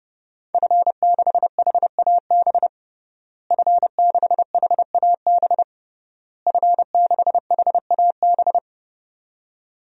Morse code, heard as a keyed tone, about 30 wpm, 700 Hz. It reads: F6HAB F6HAB F6HAB
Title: DXCC-100 Call Signs - 5 Characters - Repeated 3x - 30wpm